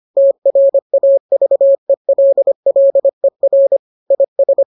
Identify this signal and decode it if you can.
Morse code, heard as a keyed tone, about 25 wpm, 550 Hz. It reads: TRAVELLER IS